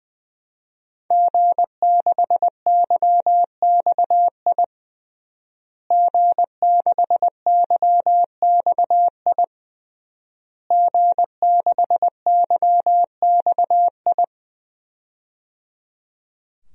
Morse code, heard as a keyed tone, about 20 wpm, 700 Hz. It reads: G6YXI G6YXI G6YXI
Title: DXCC-100 Call Signs - 5 Characters - Repeated 3x - 20wpm